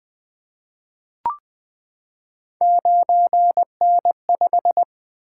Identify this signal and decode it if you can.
Morse code, heard as a keyed tone, about 20 wpm, 700 Hz. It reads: 9N5
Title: Three Letter-Number Combinations - 20wpm